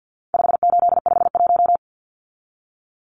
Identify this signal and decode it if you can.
Morse code, heard as a keyed tone, about 50 wpm, 700 Hz. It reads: 5851
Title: Numbers - 4-Digit 50wpm